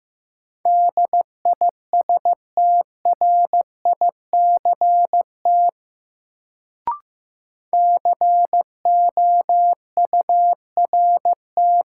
Morse code, heard as a keyed tone, about 15 wpm, 700 Hz. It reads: DISTRICT COURT
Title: Sentences from Top 1600 Words - New Word Drill - 15wpm